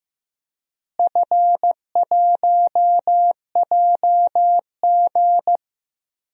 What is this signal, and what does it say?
Morse code, heard as a keyed tone, about 15 wpm, 700 Hz. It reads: F1JG
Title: DXCC-100 Call Signs - 4 Characters - Repeated 3x - 15wpm